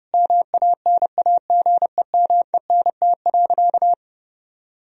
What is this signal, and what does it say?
Morse code, heard as a keyed tone, about 32 wpm, 700 Hz. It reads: MANAGEMENT.
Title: Sentences from Top 1000 Words - Review Part 4 - 30wpm